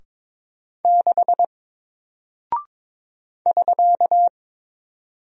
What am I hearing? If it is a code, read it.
Morse code, heard as a keyed tone, about 22 wpm, 700 Hz. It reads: 6 <SK>
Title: Single Character (Letter:Number:Punctuation:Prosign) Alt 22wpm